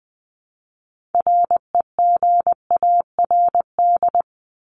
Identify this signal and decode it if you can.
Morse code, heard as a keyed tone, about 20 wpm, 700 Hz. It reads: REGARD
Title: ED Suffix - Top 100 Words with Base Words 20wpm